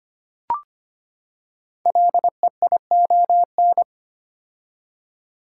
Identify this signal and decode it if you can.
Morse code, heard as a keyed tone, about 25 wpm, 700 Hz. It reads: LEION
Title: Code Groups - 5 Letter:Numbers - 25wpm